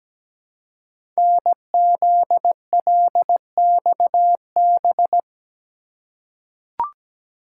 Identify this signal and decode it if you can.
Morse code, heard as a keyed tone, about 17 wpm, 700 Hz. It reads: NZLXB
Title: Code Groups - 5 Letters - 17wpm